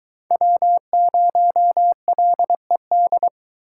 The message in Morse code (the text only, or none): W0LED